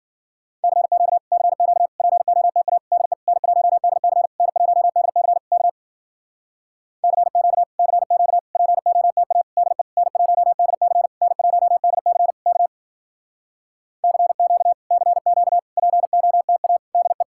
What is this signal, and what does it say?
Morse code, heard as a keyed tone, about 52 wpm, 700 Hz. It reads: CQ CQ POTA DE N1DC N1DC K CQ CQ POTA DE N1DC N1DC K CQ CQ POTA DE